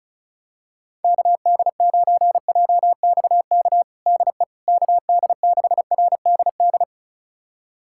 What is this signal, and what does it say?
Morse code, heard as a keyed tone, about 34 wpm, 700 Hz. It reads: KD9JXK DE KD6RDD
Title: USA Callsign DE Callsign - 6 Characters - 35wpm